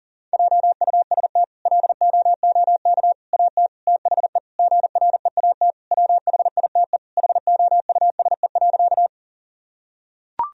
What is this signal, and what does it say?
Morse code, heard as a keyed tone, about 37 wpm, 700 Hz. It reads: JUST LOOK AT THE GREAT WHITE HOUSE.
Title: Sentences from Top 1000 Words - Review Part 1 - 40wpm